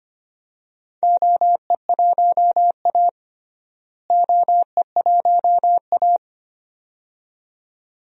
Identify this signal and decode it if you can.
Morse code, heard as a keyed tone, about 25 wpm, 700 Hz. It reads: OE1A OE1A
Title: DXCC-100 Call Signs - 4 Characters - Repeated 2x - 25wpm